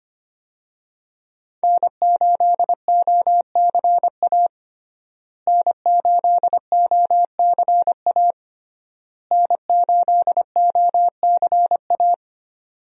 Morse code, heard as a keyed tone, about 25 wpm, 700 Hz. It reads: N8OCA N8OCA N8OCA
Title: USA Call Signs - 5 Characters - Repeated 3x - 25wpm